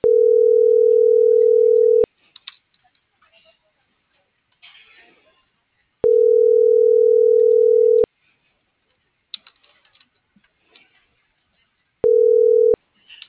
Ringback tone (North America), 440 + 480 Hz.